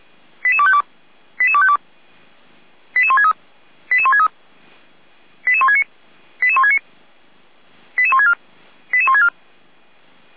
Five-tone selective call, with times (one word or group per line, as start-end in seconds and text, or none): 0.4-0.8 88252
1.4-1.8 88252
3.0-3.3 88163
3.9-4.3 88163
5.5-5.8 88169
6.4-6.8 88169
8.0-8.3 88164
8.9-9.3 88164